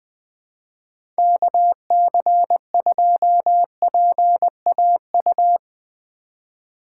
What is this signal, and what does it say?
Morse code, heard as a keyed tone, about 20 wpm, 700 Hz. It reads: KC2PAU
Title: USA Call Signs - 6 Characters - 3x Character Spacing - 20wpm